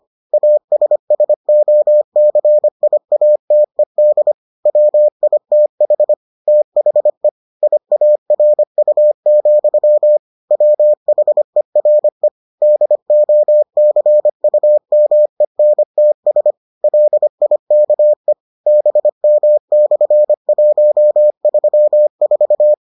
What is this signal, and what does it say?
Morse code, heard as a keyed tone, about 25 wpm, 600 Hz. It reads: ASSOCIATED WITH THE IARU, WHERE DOCUMENTS LIKE BM/134